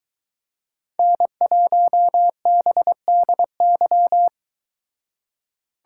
Morse code, heard as a keyed tone, about 23 wpm, 700 Hz. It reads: N1BDY